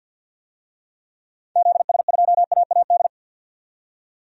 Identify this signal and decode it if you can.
Morse code, heard as a keyed tone, about 53 wpm, 700 Hz. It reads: ZS2AAD
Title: DXCC-100 Call Signs - 6 Characters - 50wpm